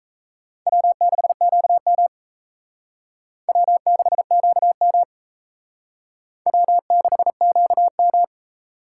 Morse code, heard as a keyed tone, about 34 wpm, 700 Hz. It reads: W6QM W6QM W6QM
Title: Speed Racing - USA 1x2 Call Signs 28wpm